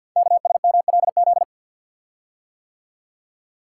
Morse code, heard as a keyed tone, about 48 wpm, 690 Hz. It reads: KSMLZ
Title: Code Groups - 5 Letters - 50wpm